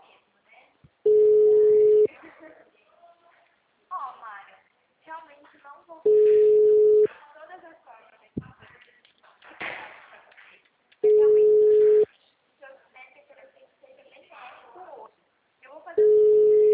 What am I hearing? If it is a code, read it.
Ringback tone (Europe, 425 Hz), with speech.
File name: out- -2030- - - .4808